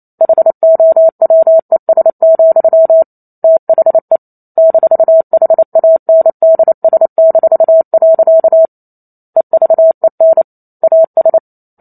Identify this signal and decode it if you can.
Morse code, heard as a keyed tone, about 28 wpm, 650 Hz. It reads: HOWES, THE -HANDS-. EVEN AS
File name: Chapter 2 - 28 WPM